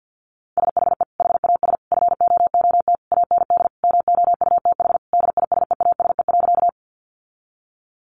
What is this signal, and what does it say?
Morse code, heard as a keyed tone, about 50 wpm, 700 Hz. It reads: SHE HAS FOOT AND MOUTH DISEASE.
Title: Sentences from Top 1500 Words 50wpm